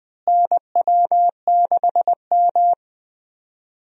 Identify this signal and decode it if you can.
Morse code, heard as a keyed tone, about 20 wpm, 700 Hz. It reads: NW6M
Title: USA Call Signs - 4 Characters - Repeated 3x - 20wpm